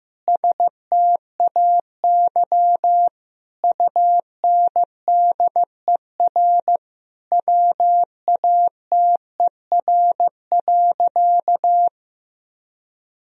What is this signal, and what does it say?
Morse code, heard as a keyed tone, about 16 wpm, 700 Hz. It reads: STAY UNDER WATER.
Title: Sentences from Top 1000 Words - Review Part 2 - 15wpm